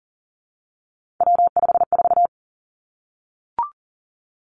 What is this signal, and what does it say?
Morse code, heard as a keyed tone, about 41 wpm, 700 Hz. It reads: W54
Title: Three Challenging Letter-Number Combinations 40wpm